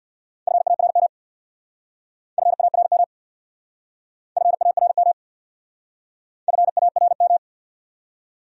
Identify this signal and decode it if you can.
Morse code, heard as a keyed tone, about 64 wpm, 700 Hz. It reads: FARM FARM FARM FARM